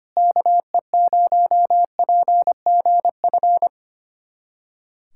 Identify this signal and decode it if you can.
Morse code, heard as a keyed tone, about 25 wpm, 700 Hz. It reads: KE0PGF
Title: USA Callsign DE Callsign - 6 Characters - 25wpm